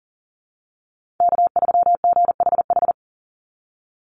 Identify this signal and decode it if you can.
Morse code, heard as a keyed tone, about 40 wpm, 700 Hz. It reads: K3GHH
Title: Call Signs Repeated 3x - 40wpm